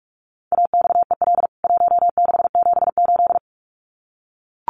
Morse code, heard as a keyed tone, about 44 wpm, 700 Hz. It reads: AXEL 1678
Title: CWT Exchange 45wpm